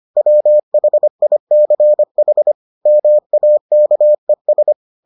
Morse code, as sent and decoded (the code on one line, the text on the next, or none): .-- .... .. -.-. .... / -- .- -.- . ...
WHICH MAKES